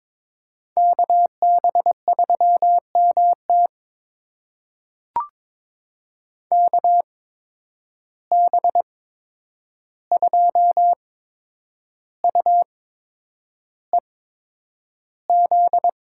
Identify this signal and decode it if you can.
Morse code, heard as a keyed tone, about 22 wpm, spaced as for 9 wpm, 700 Hz. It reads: KB3MT KB2UEZ
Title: USA Call Signs - 8x Character Spacing - 22wpm